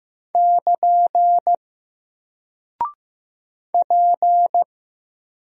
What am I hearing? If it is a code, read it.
Morse code, heard as a keyed tone, about 15 wpm, 700 Hz. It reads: <KN> P
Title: Single Character (Letter:Number:Punctuation:Prosign) Alt 15wpm